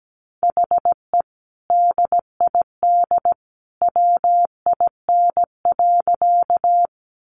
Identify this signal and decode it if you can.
Morse code, heard as a keyed tone, about 17 wpm, 700 Hz. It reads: HE DID WIN.
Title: Sentences from Top 1500 Words 17wpm